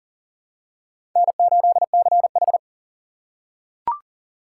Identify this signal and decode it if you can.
Morse code, heard as a keyed tone, about 40 wpm, 700 Hz. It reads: N8CH
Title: USA Call Signs - 4 Characters - 6x Character Spacing - 40wpm